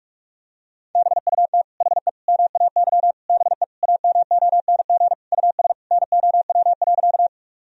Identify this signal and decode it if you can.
Morse code, heard as a keyed tone, about 46 wpm, 700 Hz. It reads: BUT HE MAY BE AMONG US NOW.
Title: Sentences from Top 1000 Words - Review Part 3 - 45wpm